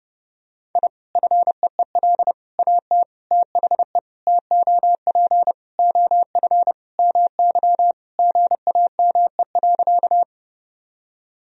Morse code, heard as a keyed tone, about 30 wpm, 700 Hz. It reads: I FEEL AT THE TOP OF MY GAME.
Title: Sentences from Top 1000 Words - Review Part 2 - 30wpm